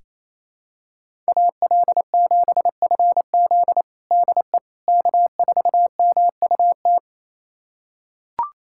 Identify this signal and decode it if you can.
Morse code, heard as a keyed tone, about 28 wpm, 700 Hz. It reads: AL7FZ DE K4MUT
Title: USA Callsign DE Callsign - 5 Characters - 28wpm